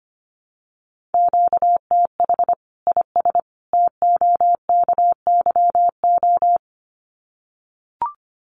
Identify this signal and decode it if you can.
Morse code, heard as a keyed tone, about 25 wpm, 700 Hz. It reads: QTH IS TOKYO
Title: Common QSO Phrases 25wpm